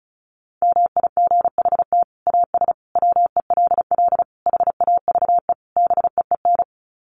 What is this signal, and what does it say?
Morse code, heard as a keyed tone, about 34 wpm, 700 Hz. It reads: MIGHT AS WELL HAVE BEEN
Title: Sets of 5 Words Encore2 35wpm